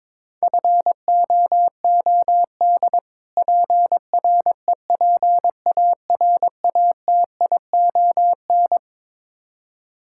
Morse code, heard as a keyed tone, about 22 wpm, 700 Hz. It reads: FOOD PREPARATION
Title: K1USN SST QSO Practice 22wpm